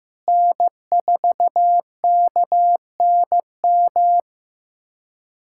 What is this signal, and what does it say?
Morse code, heard as a keyed tone, about 15 wpm, 700 Hz. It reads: N4KNM
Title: USA Callsign DE Callsign - 5 Characters - 15wpm